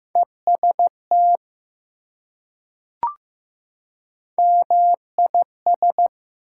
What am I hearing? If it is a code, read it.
Morse code, heard as a keyed tone, about 15 wpm, 700 Hz. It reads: EST MIS
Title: Prefixes and Suffixes 15wpm